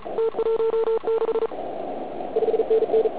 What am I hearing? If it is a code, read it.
Morse code, heard as a keyed tone, about 35 wpm, 440 Hz, with noise. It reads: T16 5NN
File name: Archiv / CUC106 / cuc106 PILEUP